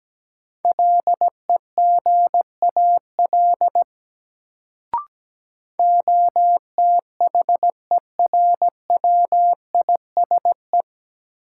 Morse code, heard as a keyed tone, about 17 wpm, 700 Hz. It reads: LEGAL OTHERWISE
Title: Sentences from Top 1200 Words - New Word Drill - 17wpm